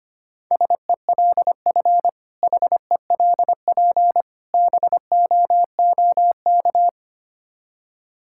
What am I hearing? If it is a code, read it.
Morse code, heard as a keyed tone, about 25 wpm, 700 Hz. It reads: SELF HELP BOOK
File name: practice / 25wpm / Difficult Words with Hints - 25wpm